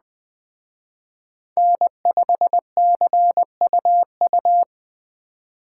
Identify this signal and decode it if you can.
Morse code, heard as a keyed tone, about 20 wpm, 700 Hz. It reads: N5CUU